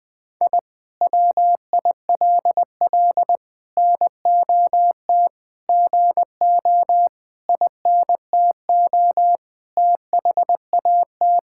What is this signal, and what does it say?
Morse code, heard as a keyed tone, about 21 wpm, 700 Hz. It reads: I WILL NOT GO INTO THAT
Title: Sentences from Top 1000 Words - Review Part 6 - 20wpm